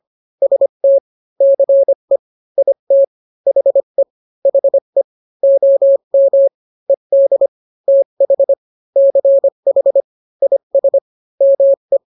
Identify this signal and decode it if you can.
Morse code, heard as a keyed tone, about 25 wpm, 550 Hz. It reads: ST CE IT HE HE OM ED TH CH IS ME